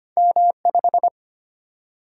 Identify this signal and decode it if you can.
Morse code, heard as a keyed tone, about 25 wpm, 700 Hz. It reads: M5
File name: Two Letter-Number Combinations - 25wpm